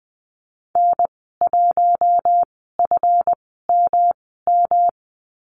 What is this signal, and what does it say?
Morse code, heard as a keyed tone, about 20 wpm, spaced as for 15 wpm, 700 Hz. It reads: N1FMM
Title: 147 Lesson 24g - Letter y - Call Signs 20wpm